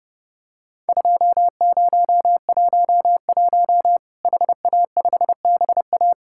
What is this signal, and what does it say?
Morse code, heard as a keyed tone, about 30 wpm, 700 Hz. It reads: 2011 HA5BA
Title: Worked All Europe DX Contest - QTC Exchange - 30wpm